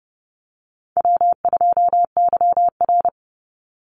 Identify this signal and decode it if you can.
Morse code, heard as a keyed tone, about 30 wpm, 700 Hz. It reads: W2YR
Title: Call Signs - 30wpm